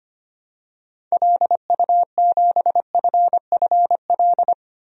Morse code, heard as a keyed tone, about 25 wpm, 690 Hz. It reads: LU7FFL
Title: DXCC-100 Call Signs - 6 Characters - Repeated 3x - 25wpm